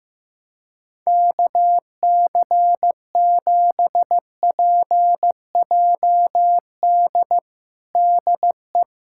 Morse code, heard as a keyed tone, about 15 wpm, 700 Hz. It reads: KC7PJD DE
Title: USA Callsign DE Callsign 15wpm